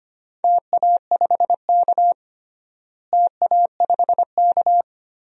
Morse code, heard as a keyed tone, about 25 wpm, 700 Hz. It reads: TA5K TA5K
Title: DXCC-100 Call Signs - 4 Characters - Repeated 2x - 25wpm